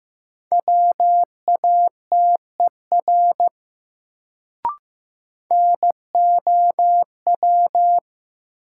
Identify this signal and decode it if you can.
Morse code, heard as a keyed tone, about 15 wpm, 700 Hz. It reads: WATER NOW
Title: Sentences from Top 100 Words - New Word Drill - 15wpm